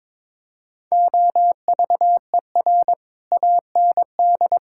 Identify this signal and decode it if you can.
Morse code, heard as a keyed tone, about 22 wpm, 700 Hz. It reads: OVER AND
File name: practice / 22wpm / Sets of 6 Words - 22wpm